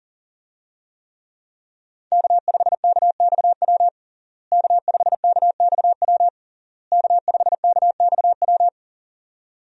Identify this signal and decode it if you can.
Morse code, heard as a keyed tone, about 40 wpm, 700 Hz. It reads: K5KXW K5KXW K5KXW
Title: USA Call Signs - 5 Characters - Repeated 3x - 40wpm